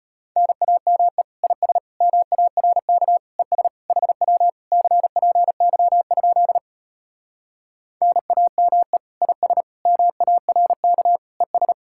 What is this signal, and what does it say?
Morse code, heard as a keyed tone, about 33 wpm, 700 Hz. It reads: NAME IS MARK ES HW CPY? NAME IS MARK ES